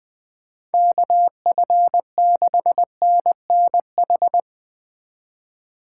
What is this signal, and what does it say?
Morse code, heard as a keyed tone, about 22 wpm, 700 Hz. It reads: KF6NNH
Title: Speed Racing - USA 2x3 Call Signs 20wpm